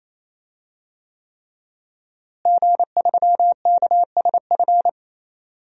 Morse code, heard as a keyed tone, about 28 wpm, 700 Hz. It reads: G3KSF
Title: DXCC-100 Call Signs - 5 Characters - Repeated 3x - 28wpm